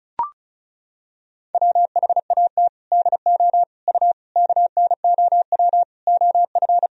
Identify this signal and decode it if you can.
Morse code, heard as a keyed tone, about 36 wpm, 700 Hz. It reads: WHAT DO U KNOW OF
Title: Sentences from Top 100 Words and Abbreviations 35wpm